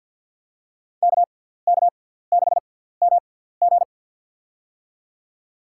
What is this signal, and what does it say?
Morse code, heard as a keyed tone, about 53 wpm, spaced as for 17 wpm, 700 Hz. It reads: KK6MG
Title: USA Call Signs - 5 Characters - 6x Character Spacing - 50wpm